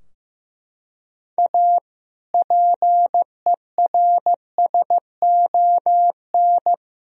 Morse code, heard as a keyed tone, about 15 wpm, 700 Hz. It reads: A PERSON
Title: Sentences from Top 1000 Words - Review Part 4 - 15wpm